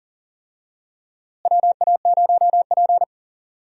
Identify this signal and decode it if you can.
Morse code, heard as a keyed tone, about 40 wpm, 700 Hz. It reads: WA0P
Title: USA Call Signs - 4 Characters - Repeated 2x - 40wpm